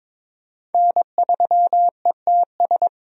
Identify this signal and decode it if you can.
Morse code, heard as a keyed tone, about 22 wpm, 700 Hz. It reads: N3ETS
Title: USA Call Signs - 5 Characters - 22wpm